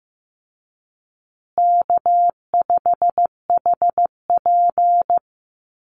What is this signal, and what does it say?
Morse code, heard as a keyed tone, about 15 wpm, 700 Hz. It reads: K5HP
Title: Easy Call Signs Repeated 3x - 15wpm